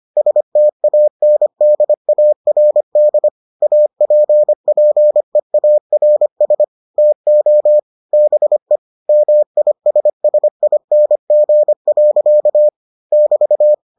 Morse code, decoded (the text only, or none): STANDARD APPEARS TO BE MISSING. <BT>